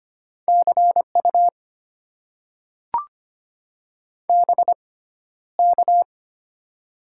Morse code, heard as a keyed tone, about 25 wpm, spaced as for 13 wpm, 700 Hz. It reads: CU BK
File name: Top 90 QSO Elements - 6x Character Spacing - 25wpm